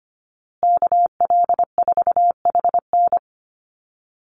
Morse code, heard as a keyed tone, about 25 wpm, 700 Hz. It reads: KL4HN
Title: USA Call Signs Repeated 3x 25wpm